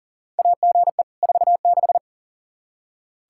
Morse code, heard as a keyed tone, about 40 wpm, 700 Hz. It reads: AGE 46
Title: Practice QSO - Exchange 3 - More Info - 40wpm